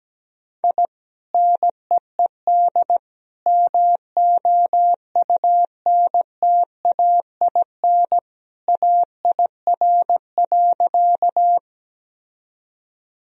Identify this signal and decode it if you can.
Morse code, heard as a keyed tone, about 17 wpm, 700 Hz. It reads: I NEED MOUNTAIN AIR.
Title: Sentences from Top 1000 Words - Review Part 1 - 17wpm